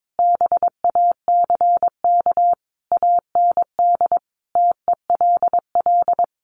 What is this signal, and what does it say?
Morse code, heard as a keyed tone, about 22 wpm, 700 Hz. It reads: BACK AND TELL
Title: Sentences from Top 1200 Words 22wpm